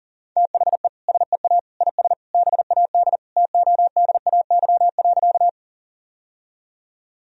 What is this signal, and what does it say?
Morse code, heard as a keyed tone, about 40 wpm, 700 Hz. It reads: THE SEA IS BAD TODAY.
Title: Sentences from Top 1000 Words - Review Part 1 - 40wpm